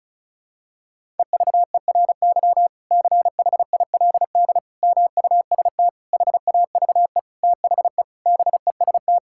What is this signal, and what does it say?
Morse code, heard as a keyed tone, about 36 wpm, 700 Hz. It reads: EVERY CHILD MUST HAVE THE BEST
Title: Sentences from Top 1000 Words - Review Part 6 - 35wpm